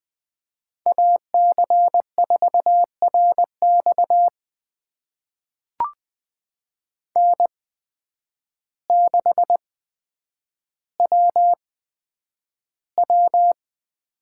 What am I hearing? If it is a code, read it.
Morse code, heard as a keyed tone, about 20 wpm, spaced as for 9 wpm, 700 Hz. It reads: AC4RX N6WW
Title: USA Call Signs - 5 Characters - 8x Character Spacing - 20wpm